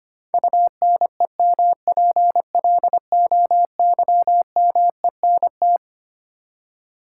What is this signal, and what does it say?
Morse code, heard as a keyed tone, about 25 wpm, 700 Hz. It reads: UNEMPLOYMENT
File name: Twelve Letter Words - 25wpm